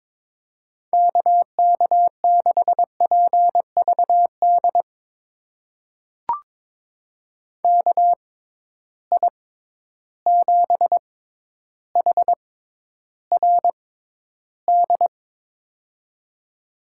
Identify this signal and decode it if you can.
Morse code, heard as a keyed tone, about 22 wpm, spaced as for 11 wpm, 700 Hz. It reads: KK6PVD KI7HRD
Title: USA Call Signs - 6x Character Spacing - 22wpm